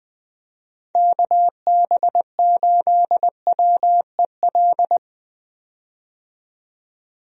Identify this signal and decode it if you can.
Morse code, heard as a keyed tone, about 20 wpm, 700 Hz. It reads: KB8WEL
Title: USA Call Signs - 6 Characters - 20wpm